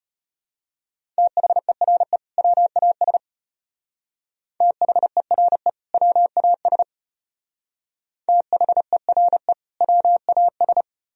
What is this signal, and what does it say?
Morse code, heard as a keyed tone, about 32 wpm, 700 Hz. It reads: THERE WAS THERE WAS THERE WAS